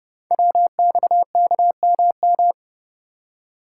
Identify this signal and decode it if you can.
Morse code, heard as a keyed tone, about 29 wpm, 700 Hz. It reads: WXKMM
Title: Code Groups - 5 Letters - 30wpm